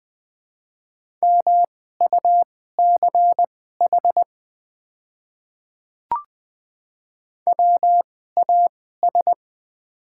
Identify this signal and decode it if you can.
Morse code, heard as a keyed tone, about 20 wpm, spaced as for 14 wpm, 700 Hz. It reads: MUCH WAS
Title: 2x Character Spacing - Top 500 Words - 20wpm